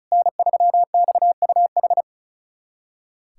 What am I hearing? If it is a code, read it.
Morse code, heard as a keyed tone, about 36 wpm, 700 Hz. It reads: N3XUH
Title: USA Callsign DE Callsign 35wpm